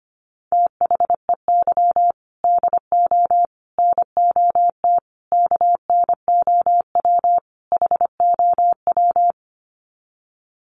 Morse code, heard as a keyed tone, about 25 wpm, 700 Hz. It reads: THEY DO NOT KNOW HOW
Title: Sets of 5 Words Encore2 25wpm